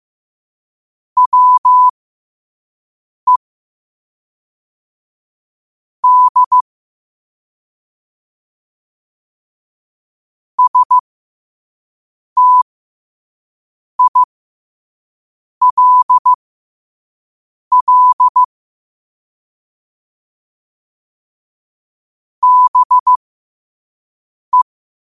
Morse code, heard as a keyed tone, about 15 wpm, spaced as for 5 wpm, 1000 Hz. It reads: WE D STILL BE
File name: qotdfiles / 5wpm